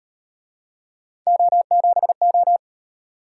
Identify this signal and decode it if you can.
Morse code, heard as a keyed tone, about 36 wpm, 700 Hz. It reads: O7O